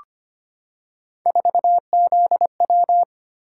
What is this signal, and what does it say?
Morse code, heard as a keyed tone, about 25 wpm, 700 Hz. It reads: 4ZW
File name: Three Letter-Number Combinations - 25wpm